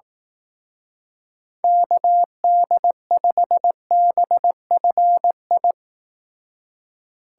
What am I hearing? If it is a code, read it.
Morse code, heard as a keyed tone, about 18 wpm, 700 Hz. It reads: KD5BFI
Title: Speed Racing - USA 2x3 Call Signs 15wpm